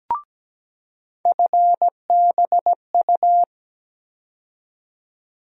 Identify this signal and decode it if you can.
Morse code, heard as a keyed tone, about 17 wpm, 700 Hz. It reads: FBU
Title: USA Call Sign Postfixes - 3 Letters 17wpm